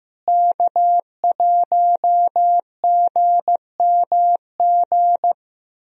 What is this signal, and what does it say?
Morse code, heard as a keyed tone, about 15 wpm, 700 Hz. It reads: K1GMG
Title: USA Call Signs - 5 Characters - Repeated 2x - 15wpm